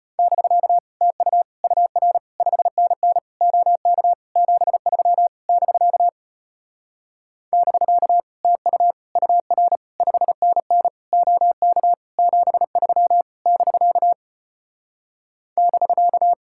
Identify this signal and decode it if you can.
Morse code, heard as a keyed tone, about 33 wpm, 700 Hz. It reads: <BK> TU UR 5NN OK 73 <BK> <BK> TU UR 5NN OK 73 <BK> <BK>